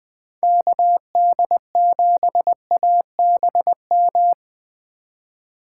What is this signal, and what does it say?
Morse code, heard as a keyed tone, about 20 wpm, 700 Hz. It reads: KD7ABM
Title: Speed Racing - USA 2x3 Call Signs 17wpm